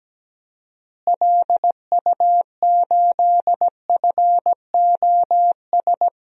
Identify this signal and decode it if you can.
Morse code, heard as a keyed tone, about 17 wpm, 700 Hz. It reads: LU8FOS
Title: DXCC-100 Call Signs - 6 Characters - Repeated 3x - 17wpm